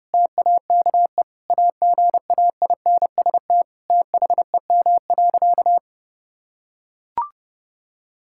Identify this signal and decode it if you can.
Morse code, heard as a keyed tone, about 32 wpm, 700 Hz. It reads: TAKE AGAINST THEM.
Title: Sentences from Top 1000 Words - Review Part 6 - 30wpm